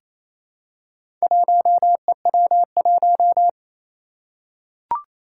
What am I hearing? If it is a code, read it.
Morse code, heard as a keyed tone, about 29 wpm, 700 Hz. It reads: 1EW1
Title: Four Challenging Letter-Number Combinations 28wpm